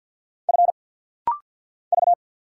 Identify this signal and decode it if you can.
Morse code, heard as a keyed tone, about 50 wpm, 700 Hz. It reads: F V
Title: Single Letter-Number Alt - Rapid-Fire 50wpm